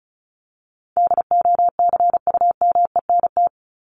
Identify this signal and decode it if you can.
Morse code, heard as a keyed tone, about 34 wpm, 700 Hz. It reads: DOCUMENT
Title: Top 1751-2000 Words 35wpm